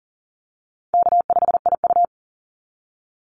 Morse code, heard as a keyed tone, about 40 wpm, 700 Hz. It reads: K5IU
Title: Call Signs Repeated 2x - 40wpm